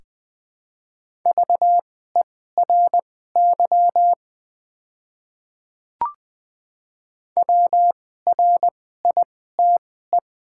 Morse code, heard as a keyed tone, about 20 wpm, spaced as for 14 wpm, 700 Hz. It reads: VERY WRITE
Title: 2x Character Spacing - Top 200 Words - 20wpm